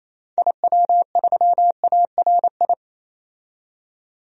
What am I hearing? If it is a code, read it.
Morse code, heard as a keyed tone, about 28 wpm, 700 Hz. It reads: IW3ARI
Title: DXCC-100 Call Signs - 6 Characters - 28wpm